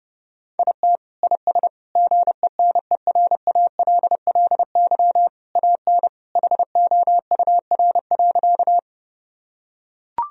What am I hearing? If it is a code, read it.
Morse code, heard as a keyed tone, about 30 wpm, 700 Hz. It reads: IT IS GENERALLY AN HOUR.
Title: Sentences from Top 1000 Words - Review Part 4 - 30wpm